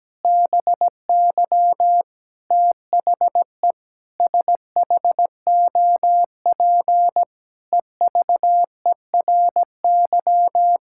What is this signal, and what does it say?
Morse code, heard as a keyed tone, about 17 wpm, 700 Hz. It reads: BY THE SHOP EVERY
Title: Sentences from Top 1600 Words 17wpm